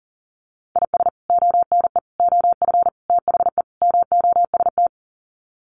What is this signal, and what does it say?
Morse code, heard as a keyed tone, about 40 wpm, 700 Hz. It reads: IS ONE OF THE MOST
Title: Sets of 5 Words 40wpm